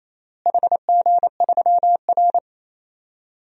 Morse code, heard as a keyed tone, about 28 wpm, 700 Hz. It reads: HG3R
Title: DXCC-100 Call Signs - 4 Characters - Repeated 2x - 28wpm